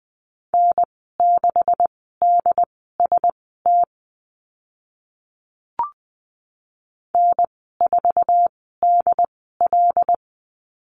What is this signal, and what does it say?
Morse code, heard as a keyed tone, about 20 wpm, spaced as for 14 wpm, 700 Hz. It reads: N6DST N4DL
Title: 140 Lesson 23g - Letter f - Call Signs 20wpm